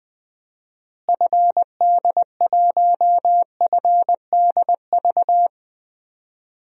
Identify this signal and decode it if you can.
Morse code, heard as a keyed tone, about 20 wpm, 700 Hz. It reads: FD1FDV
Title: DXCC-100 Call Signs - 6 Characters - Repeated 3x - 20wpm